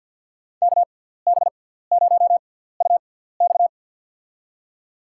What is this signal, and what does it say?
Morse code, heard as a keyed tone, about 53 wpm, spaced as for 17 wpm, 700 Hz. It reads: KB0UX
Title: USA Call Signs - 5 Characters - 6x Character Spacing - 50wpm